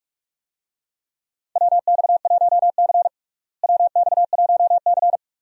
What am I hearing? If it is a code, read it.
Morse code, heard as a keyed tone, about 44 wpm, 700 Hz. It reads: WX1C WX1C
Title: USA Call Signs - 4 Characters - Repeated 2x - 45wpm